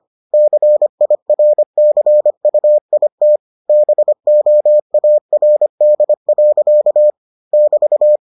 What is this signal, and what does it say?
Morse code, heard as a keyed tone, about 25 wpm, 600 Hz. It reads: CIRCUIT BOARD. <BT>